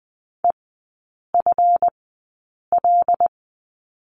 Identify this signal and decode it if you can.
Morse code, heard as a keyed tone, about 20 wpm, spaced as for 8 wpm, 700 Hz. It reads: EFL
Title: 138 Lesson 23e - Letter f - Three Characters 20wpm